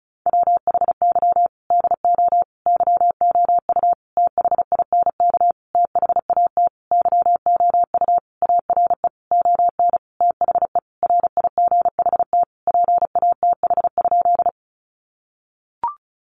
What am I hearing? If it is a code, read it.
Morse code, heard as a keyed tone, about 34 wpm, 700 Hz. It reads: WHY DO YOU THINK THAT YOU ARE ON THE RIGHT PATH?
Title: Sentences from Top 1400 Words 35wpm